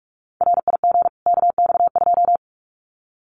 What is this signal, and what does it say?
Morse code, heard as a keyed tone, about 44 wpm, 700 Hz. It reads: RIG KX2
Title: Common QSO Phrases 45wpm